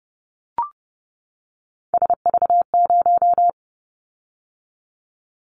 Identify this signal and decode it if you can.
Morse code, heard as a keyed tone, about 30 wpm, 700 Hz. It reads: SV0
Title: Callsign Prefixes for Top 50 DXCC - 30wpm